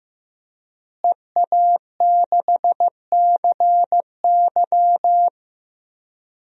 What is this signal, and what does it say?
Morse code, heard as a keyed tone, about 15 wpm, 700 Hz. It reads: EA6CY
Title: DXCC-100 Call Signs - 5 Characters - Repeated 2x - 15wpm